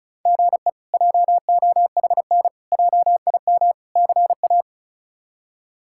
Morse code, heard as a keyed tone, about 34 wpm, 700 Hz. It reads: GE JOHN JIM CA
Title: K1USN SST QSO Practice - 35wpm